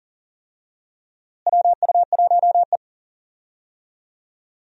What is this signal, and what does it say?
Morse code, heard as a keyed tone, about 40 wpm, 700 Hz. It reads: WU1E
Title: USA Call Signs - 4 Characters - Repeated 3x - 40wpm